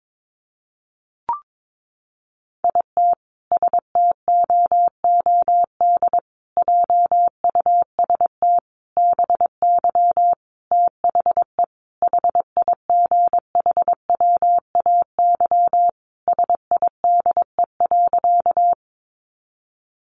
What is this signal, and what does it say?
Morse code, heard as a keyed tone, about 22 wpm, 700 Hz. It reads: IT STOOD JUST BY THE HIGHWAY SIDE.
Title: Sentences from Top 1400 Words 22wpm